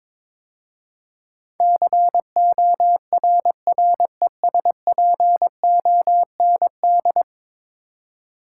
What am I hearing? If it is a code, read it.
Morse code, heard as a keyed tone, about 22 wpm, 700 Hz. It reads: CORRESPOND
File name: practice / 22wpm / 10 Letter Words - First 3 Letters Given - 22wpm